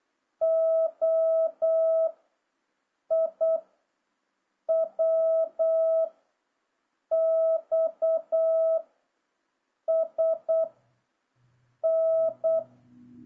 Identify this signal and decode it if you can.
Morse code, heard as a keyed tone, about 8 wpm, 650 Hz. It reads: O I W X S N
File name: jh8cbh / 2026cw